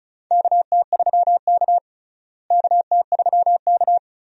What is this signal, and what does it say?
Morse code, heard as a keyed tone, about 34 wpm, 700 Hz. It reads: KT3K KT3K
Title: USA Call Signs - 4 Characters - Repeated 2x - 35wpm